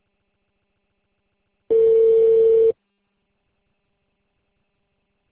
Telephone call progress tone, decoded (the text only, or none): ringback tone